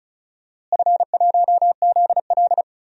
Morse code, heard as a keyed tone, about 36 wpm, 700 Hz. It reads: F1ZL
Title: DXCC-100 Call Signs - 4 Characters - Repeated 2x - 35wpm